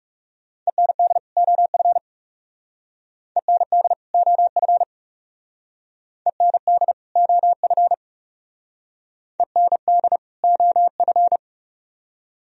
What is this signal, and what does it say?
Morse code, heard as a keyed tone, about 36 wpm, 700 Hz. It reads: END OF END OF END OF END OF
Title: Speed-Racing - Sets of 2 Words 30wpm